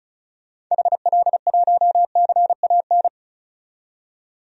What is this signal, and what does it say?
Morse code, heard as a keyed tone, about 36 wpm, 700 Hz. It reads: HL1CAN
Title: DXCC-100 Call Signs - 6 Characters - 35wpm